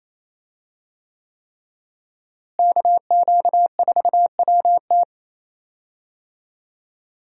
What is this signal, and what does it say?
Morse code, heard as a keyed tone, about 28 wpm, 700 Hz. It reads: KQ4WT
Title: USA Call Signs - 5 Characters - Repeated 3x - 28wpm